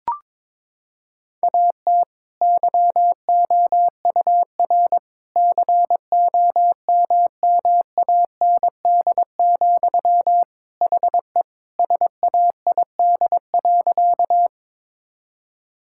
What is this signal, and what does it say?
Morse code, heard as a keyed tone, about 22 wpm, 700 Hz. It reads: AT YOUR COMMAND, HE SAID.
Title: Sentences from Top 1600 Words 22wpm